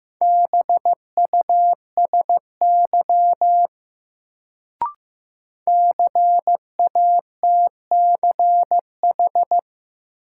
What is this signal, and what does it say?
Morse code, heard as a keyed tone, about 15 wpm, 700 Hz. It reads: BUSY CATCH
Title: Sentences from Top 1500 Words - New Word Drill Alt - 15wpm